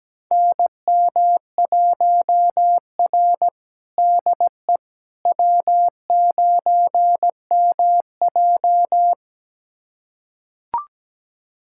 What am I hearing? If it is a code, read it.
Morse code, heard as a keyed tone, about 17 wpm, 700 Hz. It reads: NM1R DE W9MJ
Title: USA Callsign DE Callsign - 4 Characters - 17wpm